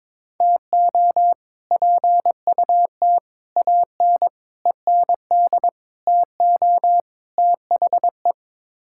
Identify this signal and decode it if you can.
Morse code, heard as a keyed tone, about 22 wpm, 700 Hz. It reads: TO PUT AN END TO THE
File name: practice / 22wpm / Sets of 6 Words - 22wpm